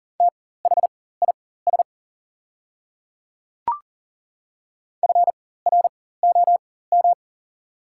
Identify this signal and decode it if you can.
Morse code, heard as a keyed tone, about 39 wpm, spaced as for 19 wpm, 690 Hz. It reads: THIS FROM
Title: 4x Character Spacing - Top 500 Words - 40wpm